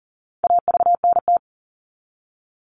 Morse code, heard as a keyed tone, about 40 wpm, 700 Hz. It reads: AVNT
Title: All Cut Numbers - 4-Digit 40wpm